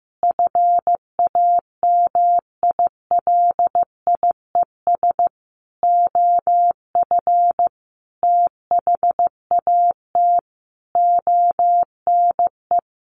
Morse code, heard as a keyed tone, about 15 wpm, 700 Hz. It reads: FAMILIES OF THAT ONE
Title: A Sound Of Thunder 15wpm